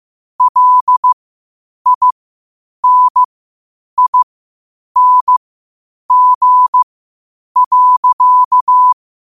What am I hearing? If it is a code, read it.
Morse code, heard as a keyed tone, about 15 wpm, spaced as for 8 wpm, 1000 Hz. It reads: LINING.
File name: qotdfiles / 8wpm